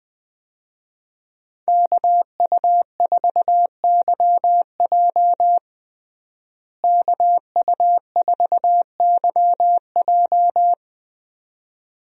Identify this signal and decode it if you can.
Morse code, heard as a keyed tone, about 21 wpm, 700 Hz. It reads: KU4YJ KU4YJ
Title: USA Call Signs - 5 Characters - Repeated 2x - 20wpm